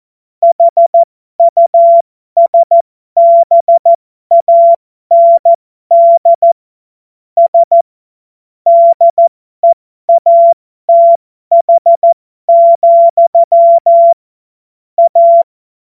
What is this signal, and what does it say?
Morse code, heard as a keyed tone, about 14 wpm, spaced as for 12 wpm, 680 Hz. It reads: HUSBAND S DEATH, A
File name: Morsekurs / Witze